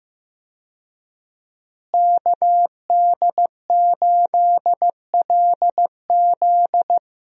Morse code, heard as a keyed tone, about 15 wpm, 700 Hz. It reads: KD8LZ
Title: USA Call Signs - 5 Characters - Repeated 3x - 15wpm